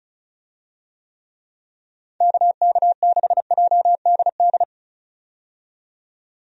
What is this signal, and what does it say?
Morse code, heard as a keyed tone, about 36 wpm, 700 Hz. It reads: KK6JDD
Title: USA Call Signs - 6 Characters - 35wpm